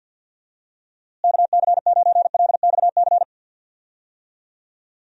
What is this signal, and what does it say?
Morse code, heard as a keyed tone, about 50 wpm, 700 Hz. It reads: KC9LXC
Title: USA Call Signs - 6 Characters - Repeated 2x - 50wpm